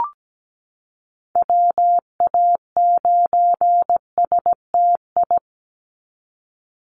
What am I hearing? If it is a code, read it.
Morse code, heard as a keyed tone, about 17 wpm, 700 Hz. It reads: WA9STI
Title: NA SOTA Activator Callsigns 17wpm